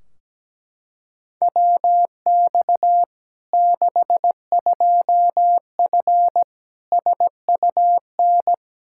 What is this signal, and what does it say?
Morse code, heard as a keyed tone, about 17 wpm, 700 Hz. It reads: WX 62F SUN
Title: Practice QSO - Exchange 2 - Info - 17wpm